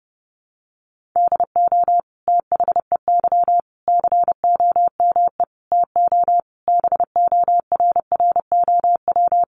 Morse code, heard as a keyed tone, about 30 wpm, 700 Hz. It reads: DO THEY COME TO BORROW
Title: Sentences from Top 1500 Words 30wpm